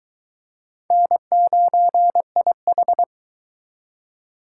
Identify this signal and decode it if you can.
Morse code, heard as a keyed tone, about 22 wpm, 700 Hz. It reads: N9IH